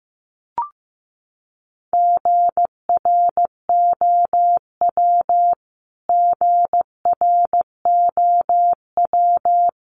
Morse code, heard as a keyed tone, about 15 wpm, 700 Hz. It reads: GROW GROW
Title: Top 500 Words Repeated 15wpm